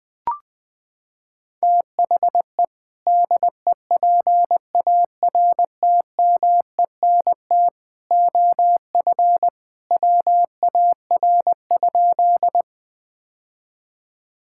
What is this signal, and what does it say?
Morse code, heard as a keyed tone, about 21 wpm, 700 Hz. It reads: THE DEPARTMENT OF WAR?
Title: Sentences from Top 1000 Words - Review Part 5 - 20wpm